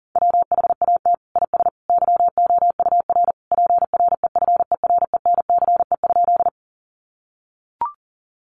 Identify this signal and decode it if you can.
Morse code, heard as a keyed tone, about 38 wpm, 700 Hz. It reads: WHAT IS YOUR PREFERENCE?
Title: Sentences from Top 1500 Words 40wpm